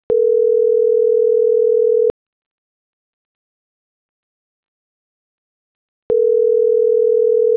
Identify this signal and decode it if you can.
Ringback tone (North America), 440 + 480 Hz.